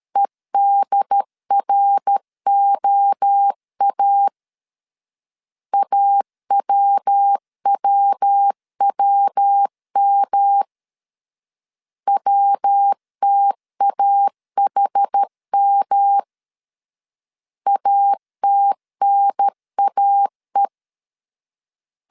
Morse code, heard as a keyed tone, about 13 wpm, 800 Hz. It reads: EDROA AWWWM WTAHM ATNAE
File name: CW / DET / 1 3 1500C S C 12wpm 2750